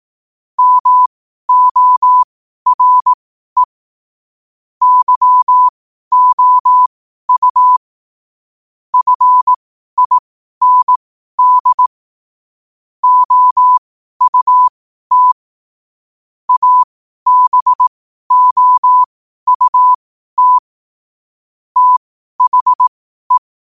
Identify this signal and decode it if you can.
Morse code, heard as a keyed tone, about 18 wpm, spaced as for 12 wpm, 1000 Hz. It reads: MORE YOU FIND OUT ABOUT THE
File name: qotdfiles / 12wpm